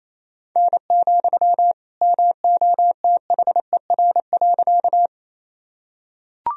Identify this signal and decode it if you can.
Morse code, heard as a keyed tone, about 28 wpm, 700 Hz. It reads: N, MOTHER.
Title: Sentences from Top 200 Words and Abbreviations 28wpm